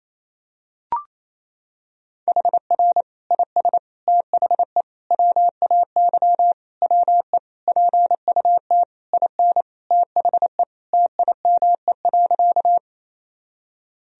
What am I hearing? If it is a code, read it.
Morse code, heard as a keyed tone, about 28 wpm, 700 Hz. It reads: HR IS THE WAY WE PUT IN THE TIME.
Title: Sentences from Top 200 Words and Abbreviations 28wpm